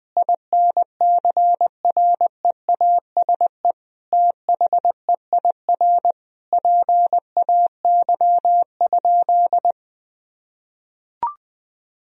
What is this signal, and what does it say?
Morse code, heard as a keyed tone, about 20 wpm, 700 Hz. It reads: INCREASE THEIR PAY?
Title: Sentences from Top 1000 Words - Review Part 4 - 20wpm